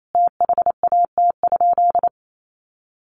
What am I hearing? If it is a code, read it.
Morse code, heard as a keyed tone, about 28 wpm, 700 Hz. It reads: THAT?
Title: Sentences from Top 1200 Words 28wpm